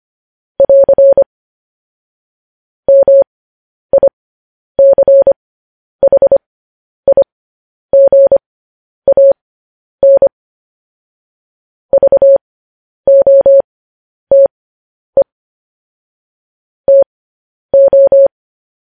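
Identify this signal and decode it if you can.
Morse code, heard as a keyed tone, about 25 wpm, spaced as for 10 wpm, 550 Hz. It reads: <AR> MICHIGAN VOTE TO
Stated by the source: Morse code news headlines practice at 10 wpm.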